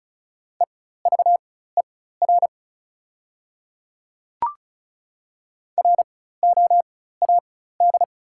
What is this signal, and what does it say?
Morse code, heard as a keyed tone, about 35 wpm, spaced as for 16 wpm, 690 Hz. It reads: EVER ROAD
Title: 4x Character Spacing - Top 500 Words - 35wpm